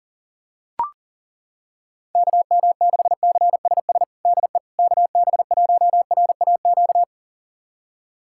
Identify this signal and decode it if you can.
Morse code, heard as a keyed tone, about 40 wpm, 700 Hz. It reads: KM6CSS DE KB1RAQ
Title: USA Callsign DE Callsign 40wpm